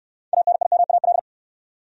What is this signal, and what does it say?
Morse code, heard as a keyed tone, about 64 wpm, 700 Hz. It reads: ANIMAL